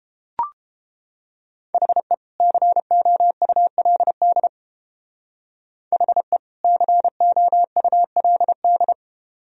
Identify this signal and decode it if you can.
Morse code, heard as a keyed tone, about 31 wpm, 700 Hz. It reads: HE COULD HE COULD